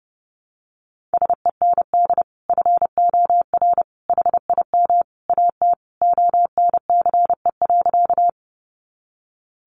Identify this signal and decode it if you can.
Morse code, heard as a keyed tone, about 30 wpm, 700 Hz. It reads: SEND FOR HIM AT ONCE.
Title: Sentences from Top 1500 Words 30wpm